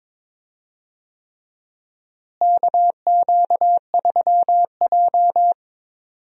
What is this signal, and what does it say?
Morse code, heard as a keyed tone, about 22 wpm, 700 Hz. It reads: KQ3J
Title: USA Call Signs - 4 Characters - Repeated 3x - 22wpm